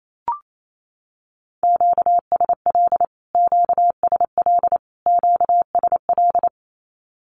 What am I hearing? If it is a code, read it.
Morse code, heard as a keyed tone, about 28 wpm, 700 Hz. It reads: QSL QSL QSL
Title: Q-Codes Repeated 3x 28wpm